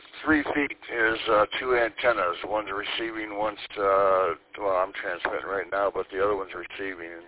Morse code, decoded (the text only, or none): none